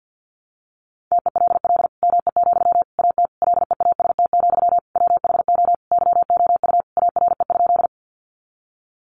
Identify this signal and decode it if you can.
Morse code, heard as a keyed tone, about 50 wpm, 700 Hz. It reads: TELL ME, AT LEAST, WHO YOU ARE?
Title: Sentences from Top 700 Words 50wpm